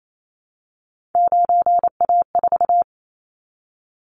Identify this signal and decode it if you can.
Morse code, heard as a keyed tone, about 28 wpm, 700 Hz. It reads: 9A4
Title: Callsign Prefixes for Top 50 DXCC - 28wpm